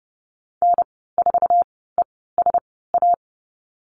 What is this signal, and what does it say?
Morse code, heard as a keyed tone, about 30 wpm, spaced as for 17 wpm, 700 Hz. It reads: N4ESA
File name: Lesson 14g - Number 2 - Call Signs 30wpm